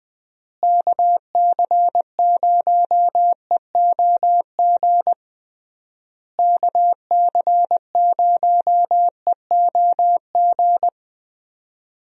Morse code, heard as a keyed tone, about 20 wpm, 700 Hz. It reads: KC0EOG KC0EOG
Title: USA Call Signs - 6 Characters - Repeated 3x - 20wpm